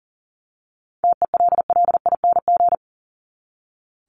Morse code, heard as a keyed tone, about 40 wpm, 700 Hz. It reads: TELLING
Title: ING Suffix - Top 100 Words with Base Words 40wpm